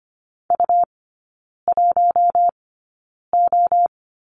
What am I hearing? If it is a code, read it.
Morse code, heard as a keyed tone, about 25 wpm, spaced as for 9 wpm, 700 Hz. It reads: U1O
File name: Lesson 17e - Letter u - Three Characters 25wpm